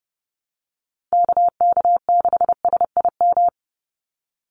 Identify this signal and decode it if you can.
Morse code, heard as a keyed tone, about 30 wpm, 700 Hz. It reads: KK6SIM
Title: USA Call Signs 30wpm